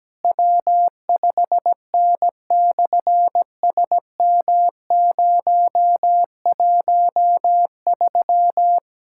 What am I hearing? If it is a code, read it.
Morse code, heard as a keyed tone, about 17 wpm, 700 Hz. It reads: W5N/SM013
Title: SOTA QSO Practice 17wpm